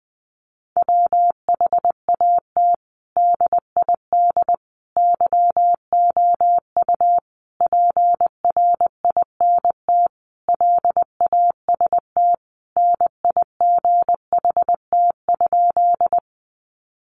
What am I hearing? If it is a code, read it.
Morse code, heard as a keyed tone, about 20 wpm, 700 Hz. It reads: WHAT DID YOU PRINT LAST NIGHT?
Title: Sentences from Top 1400 Words 20wpm